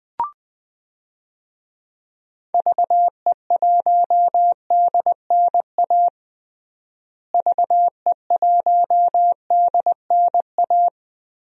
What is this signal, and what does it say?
Morse code, heard as a keyed tone, about 20 wpm, 700 Hz. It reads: VE1DNA VE1DNA
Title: DXCC-100 Call Signs - 6 Characters - Repeated 3x - 20wpm